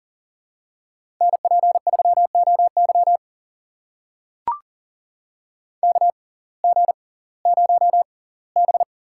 Morse code, heard as a keyed tone, about 40 wpm, spaced as for 22 wpm, 700 Hz. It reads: NP3OY KG0B
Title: USA Call Signs - 5 Characters - 6x Character Spacing - 40wpm